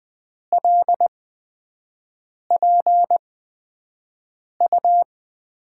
Morse code, heard as a keyed tone, about 20 wpm, spaced as for 5 wpm, 700 Hz. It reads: LPU
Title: USA Call Signs - 5 Characters - 8x Character Spacing - 20wpm